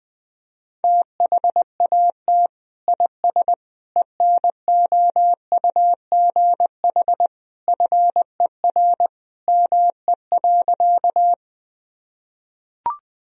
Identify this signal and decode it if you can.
Morse code, heard as a keyed tone, about 20 wpm, 700 Hz. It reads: THAT IS ENOUGH FER ME.
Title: Sentences from Top 400 Words and Abbreviations 20wpm